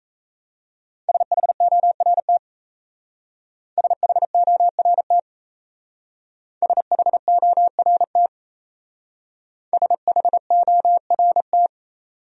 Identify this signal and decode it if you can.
Morse code, heard as a keyed tone, about 36 wpm, 690 Hz. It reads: SHORT SHORT SHORT SHORT